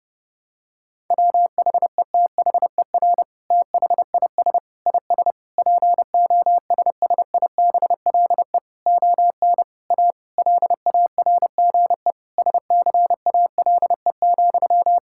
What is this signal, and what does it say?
Morse code, heard as a keyed tone, about 32 wpm, 700 Hz. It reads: WHETHER THIS IS POSSIBLE ON A LARGE SCALE,
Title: Sentences from Top 1000 Words - Review Part 5 - 30wpm